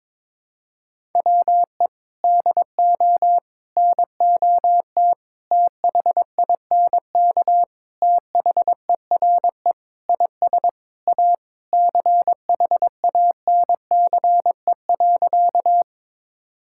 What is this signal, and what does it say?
Morse code, heard as a keyed tone, about 22 wpm, 700 Hz. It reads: WE DO NOT THINK THERE IS A CHANCE.
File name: practice / 22wpm / Sentences with 4-Word Phrases - 22wpm